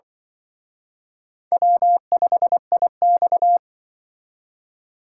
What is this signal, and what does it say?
Morse code, heard as a keyed tone, about 25 wpm, 700 Hz. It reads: W5IX
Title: Speed Racing - USA 1x2 Call Signs 20wpm